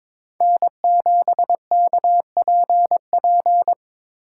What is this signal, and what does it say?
Morse code, heard as a keyed tone, about 22 wpm, 700 Hz. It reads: N7KPP
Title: USA Call Signs - 5 Characters - 3x Character Spacing - 22wpm